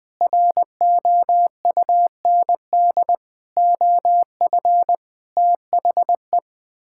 Morse code, heard as a keyed tone, about 20 wpm, 700 Hz. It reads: ROUND OF THE
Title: Sets of 6 Words - 20wpm